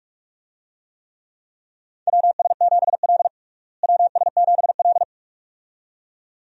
Morse code, heard as a keyed tone, about 42 wpm, 700 Hz. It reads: WS7L WS7L
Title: USA Call Signs - 4 Characters - Repeated 2x - 45wpm